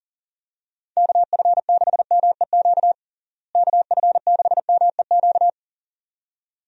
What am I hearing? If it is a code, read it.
Morse code, heard as a keyed tone, about 40 wpm, 700 Hz. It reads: KF6MEQ KF6MEQ
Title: USA Call Signs - 6 Characters - Repeated 2x - 40wpm